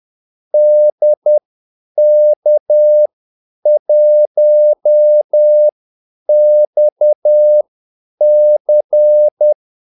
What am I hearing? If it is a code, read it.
Morse code, heard as a keyed tone, about 10 wpm, spaced as for 8 wpm, 600 Hz. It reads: DK1XC
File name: Callsigns-10-8wpm